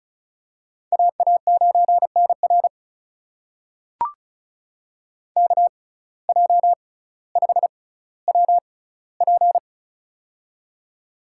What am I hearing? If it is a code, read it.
Morse code, heard as a keyed tone, about 36 wpm, spaced as for 18 wpm, 700 Hz. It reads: AA9NR KJ5WP
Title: USA Call Signs - 5 Characters - 6x Character Spacing - 35wpm